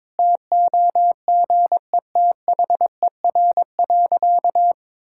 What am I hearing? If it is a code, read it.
Morse code, heard as a keyed tone, about 22 wpm, 700 Hz. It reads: TOGETHER.
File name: practice / 22wpm / Sentences with 4-Word Phrases - 22wpm